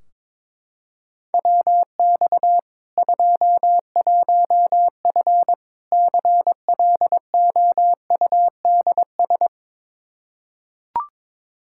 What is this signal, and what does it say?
Morse code, heard as a keyed tone, about 22 wpm, 700 Hz. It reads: WX 21F CLOUDS
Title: K1USN SST QSO Practice 22wpm